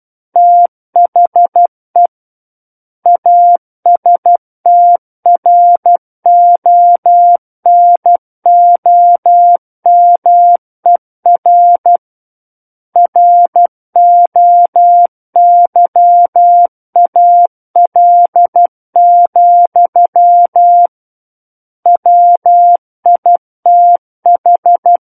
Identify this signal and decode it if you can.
Morse code, heard as a keyed tone, about 12 wpm, 700 Hz. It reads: THE ASTRONOMER ROYAL, WITH
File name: Book One / 03-Chapter 3 - 12 WPM